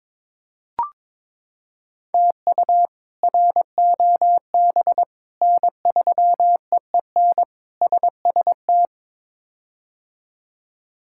Morse code, heard as a keyed tone, about 22 wpm, 700 Hz. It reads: TU ROB N3EEN SST
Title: K1USN SST QSO Practice - 22wpm